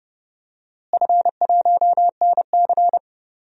Morse code, heard as a keyed tone, about 30 wpm, 700 Hz. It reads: F1NC
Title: DXCC-100 Call Signs - 4 Characters - Repeated 2x - 30wpm